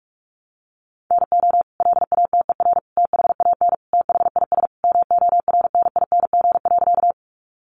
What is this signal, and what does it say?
Morse code, heard as a keyed tone, about 44 wpm, 700 Hz. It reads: NO LATER THAN THIS MORNING.
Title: Sentences from Top 700 Words 45wpm